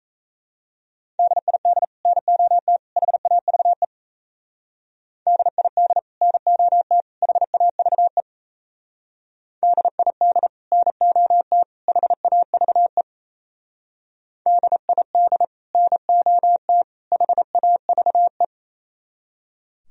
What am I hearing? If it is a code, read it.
Morse code, heard as a keyed tone, about 37 wpm, 700 Hz. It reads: DID NOT HAVE DID NOT HAVE DID NOT HAVE DID NOT HAVE